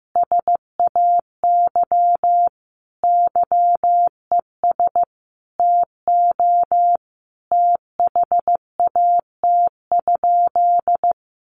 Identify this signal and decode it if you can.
Morse code, heard as a keyed tone, about 15 wpm, 700 Hz. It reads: SAY YES TO THAT?
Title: Sentences from Top 1100 Words 15wpm